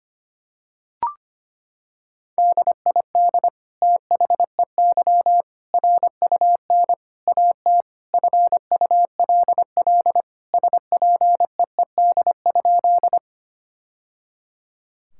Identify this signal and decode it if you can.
Morse code, heard as a keyed tone, about 25 wpm, 700 Hz. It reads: DID THEY RUN AT FULL SPEED?
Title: Sentences from Top 1000 Words - Review Part 5 - 25wpm